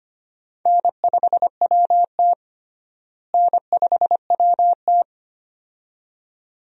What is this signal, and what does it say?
Morse code, heard as a keyed tone, about 25 wpm, 700 Hz. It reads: N5WT N5WT
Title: USA Call Signs - 4 Characters - Repeated 2x - 25wpm